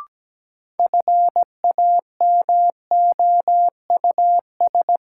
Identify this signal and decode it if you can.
Morse code, heard as a keyed tone, about 17 wpm, 700 Hz. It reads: FAMOUS
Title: Sentences from Top 1500 Words - New Word Drill Alt - 17wpm